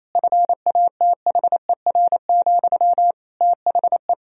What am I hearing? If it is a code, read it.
Morse code, heard as a keyed tone, about 27 wpm, 700 Hz. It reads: FATHER, THE
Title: Sentences from Top 1000 Words - Review Part 3 - 28wpm